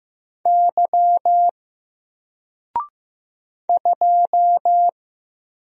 Morse code, heard as a keyed tone, about 15 wpm, 700 Hz. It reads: Y 2
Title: Single Character (Letter:Number:Punctuation:Prosign) 15wpm